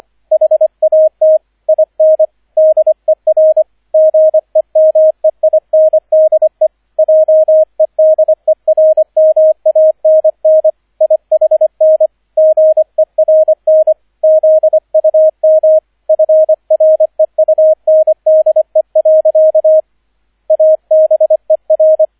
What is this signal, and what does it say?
Morse code, heard as a keyed tone, about 25 wpm, 630 Hz, with radio noise. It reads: HAT IN DER GEMEINDE JEDERMANN IHN GERN ZUM FREUNDE. ABER